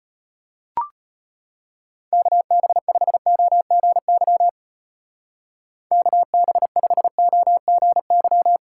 Morse code, heard as a keyed tone, about 32 wpm, 700 Hz. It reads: KB5OGY KB5OGY